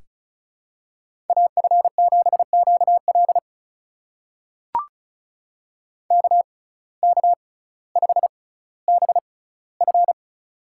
Morse code, heard as a keyed tone, about 36 wpm, spaced as for 18 wpm, 700 Hz. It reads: AF7QL KK5BF
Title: USA Call Signs - 5 Characters - 6x Character Spacing - 35wpm